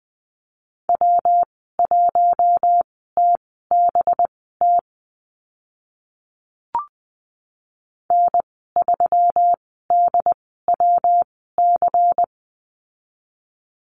Morse code, heard as a keyed tone, about 19 wpm, spaced as for 14 wpm, 700 Hz. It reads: W1TBT N3DWC
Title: 186 Lesson 30g - Letter b - Call Signs 20wpm